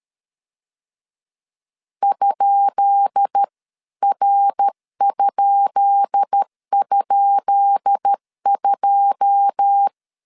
Morse code, heard as a keyed tone, about 13 wpm, 800 Hz. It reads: ?R??2
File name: LEZIONI / 10 SIGNS / 10B INT 12wpm 2414